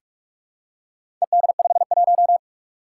Morse code, heard as a keyed tone, about 46 wpm, 700 Hz. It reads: ED51
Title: Four Challenging Letter-Number Combinations 45wpm